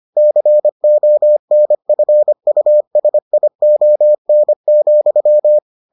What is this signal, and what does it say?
Morse code, heard as a keyed tone, about 25 wpm, 600 Hz. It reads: CONFUSION,